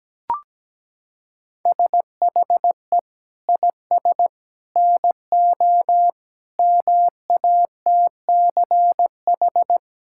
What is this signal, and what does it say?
Morse code, heard as a keyed tone, about 17 wpm, 700 Hz. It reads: SHE IS NO MATCH
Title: Sentences from Top 1600 Words 17wpm